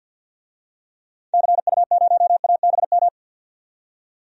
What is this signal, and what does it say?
Morse code, heard as a keyed tone, about 52 wpm, 700 Hz. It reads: CU0ABM
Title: DXCC-100 Call Signs - 6 Characters - 50wpm